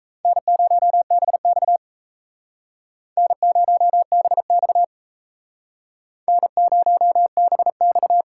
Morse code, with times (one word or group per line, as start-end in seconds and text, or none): 0.2-1.8 N0BX
3.2-4.8 N0BX
6.3-8.2 N0BX